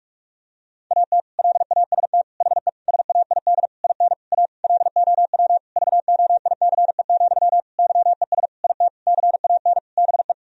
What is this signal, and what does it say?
Morse code, heard as a keyed tone, about 41 wpm, 700 Hz. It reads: AT LAST HE SAID IN A LOW VOICE, YES IT CAN BE
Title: Sentences from Top 1000 Words - Review Part 1 - 45wpm